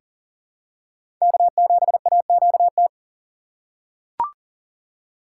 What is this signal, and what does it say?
Morse code, heard as a keyed tone, about 40 wpm, 700 Hz. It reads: K7AQT
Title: USA Call Signs - 5 Characters - 2x Character Spacing - 40wpm